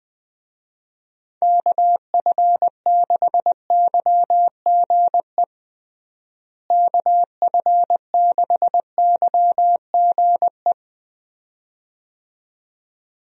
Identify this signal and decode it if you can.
Morse code, heard as a keyed tone, about 20 wpm, 700 Hz. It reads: KF6YGE KF6YGE
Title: USA Call Signs - 6 Characters - Repeated 2x - 20wpm